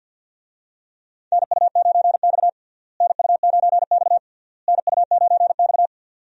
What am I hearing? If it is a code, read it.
Morse code, heard as a keyed tone, about 52 wpm, 700 Hz. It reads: NU9X NU9X NU9X
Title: USA Call Signs - 4 Characters - Repeated 3x - 50wpm